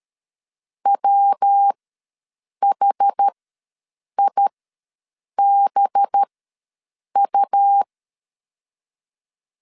Morse code, heard as a keyed tone, about 13 wpm, spaced as for 7 wpm, 800 Hz. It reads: WHIBU
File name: LEZIONI / 9 2 ws 12wpm 3406